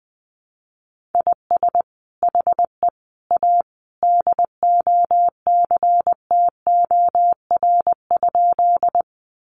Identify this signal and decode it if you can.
Morse code, heard as a keyed tone, about 20 wpm, 700 Hz. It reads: IS HE A DOCTOR?
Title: Sentences from Top 1500 Words 20wpm